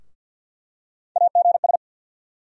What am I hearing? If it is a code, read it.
Morse code, heard as a keyed tone, about 52 wpm, 700 Hz. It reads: AGS
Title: USA Call Sign Postfixes - 3 Letters 50wpm